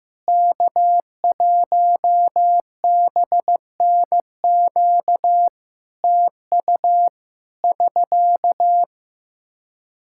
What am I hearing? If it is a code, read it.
Morse code, heard as a keyed tone, about 15 wpm, 700 Hz. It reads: K1BNQ TU <SK>
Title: Practice QSO - Exchange 4 - Ending - 15wpm